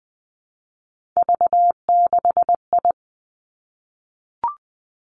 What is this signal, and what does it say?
Morse code, heard as a keyed tone, about 19 wpm, 700 Hz. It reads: V6I
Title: Three Challenging Letter-Number Combinations 20wpm